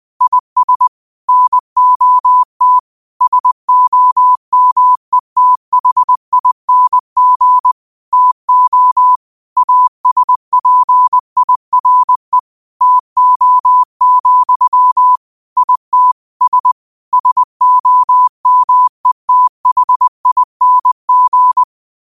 Morse code, heard as a keyed tone, about 20 wpm, 1000 Hz. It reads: IS NOT SOMETHING TO ASPIRE TO, IT S SOMETHING
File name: qotdfiles / 20wpm